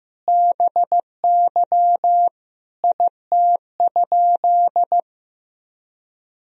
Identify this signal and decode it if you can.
Morse code, heard as a keyed tone, about 16 wpm, 700 Hz. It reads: BY IT?
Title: Sentences from Top 100 Words and Abbreviations 15wpm